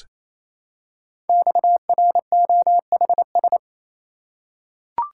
Morse code, heard as a keyed tone, about 27 wpm, 700 Hz. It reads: XROHS
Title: Code Groups - 5 Letters - 28wpm